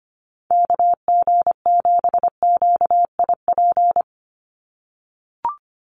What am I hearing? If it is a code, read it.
Morse code, heard as a keyed tone, about 25 wpm, 700 Hz. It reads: KG7QIP
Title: USA Call Signs 25wpm